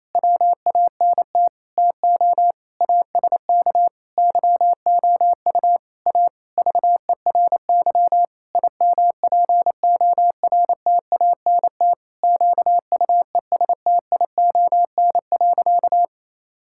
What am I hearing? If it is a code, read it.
Morse code, heard as a keyed tone, about 28 wpm, 700 Hz. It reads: WANT TO ASK YOU A VERY IMPORTANT QUESTION.
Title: Sentences from Top 1000 Words - Review Part 6 - 28wpm